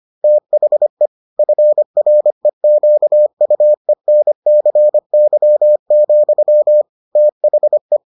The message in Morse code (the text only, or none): THE FREQUENCY, THE